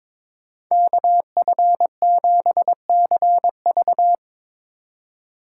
Morse code, heard as a keyed tone, about 21 wpm, 700 Hz. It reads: KF7CV
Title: USA Call Signs - 5 Characters - 2x Character Spacing - 22wpm